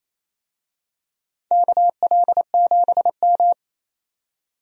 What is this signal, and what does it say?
Morse code, heard as a keyed tone, about 28 wpm, 700 Hz. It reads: KL7M
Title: DXCC-100 Call Signs - 4 Characters - 28wpm